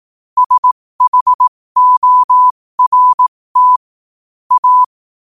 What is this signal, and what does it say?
Morse code, heard as a keyed tone, about 18 wpm, spaced as for 15 wpm, 1000 Hz. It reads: SHORT A
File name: qotdfiles / 15wpm